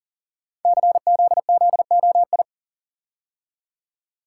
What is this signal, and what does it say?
Morse code, heard as a keyed tone, about 38 wpm, 690 Hz. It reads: CZZOI
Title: Code Groups - 5 Letters - 40wpm